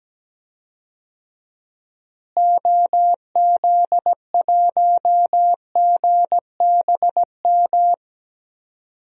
Morse code, heard as a keyed tone, about 17 wpm, 700 Hz. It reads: OZ1GBM
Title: DXCC-100 Call Signs - 6 Characters - Repeated 3x - 17wpm